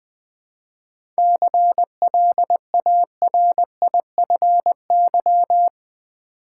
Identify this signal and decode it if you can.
Morse code, heard as a keyed tone, about 20 wpm, 700 Hz. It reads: CLARIFY
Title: 7 Letter Words - First 3 Letters Given - 20wpm